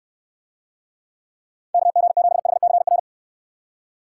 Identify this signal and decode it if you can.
Morse code, heard as a keyed tone, about 64 wpm, 700 Hz. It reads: KG7HCR